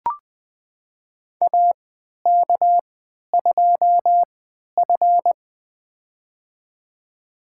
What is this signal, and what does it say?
Morse code, heard as a keyed tone, about 20 wpm, spaced as for 11 wpm, 700 Hz. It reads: AK2F
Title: USA Call Signs - 4 Characters - 3x Character Spacing - 20wpm